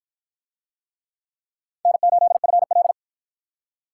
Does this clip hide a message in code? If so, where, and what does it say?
Yes: Morse code from 1.9 to 2.9 s, heard as a keyed tone, about 48 wpm, 700 Hz. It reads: N8FL